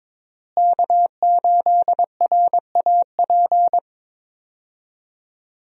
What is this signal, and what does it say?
Morse code, heard as a keyed tone, about 22 wpm, 700 Hz. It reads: K8RAP
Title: USA Call Signs - 5 Characters - Repeated 3x - 22wpm